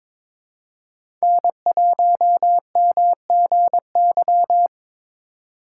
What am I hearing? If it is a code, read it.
Morse code, heard as a keyed tone, about 22 wpm, 700 Hz. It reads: N1MGY
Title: USA Call Signs - 4x Character Spacing - 22wpm